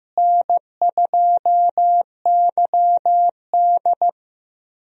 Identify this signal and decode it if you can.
Morse code, heard as a keyed tone, about 15 wpm, 700 Hz. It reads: N2YD
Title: USA Call Signs - 4 Characters - Repeated 3x - 15wpm